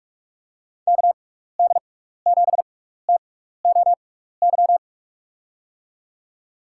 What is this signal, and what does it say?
Morse code, heard as a keyed tone, about 42 wpm, spaced as for 15 wpm, 700 Hz. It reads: KD7TOY
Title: USA Call Signs - 6x Character Spacing - 45wpm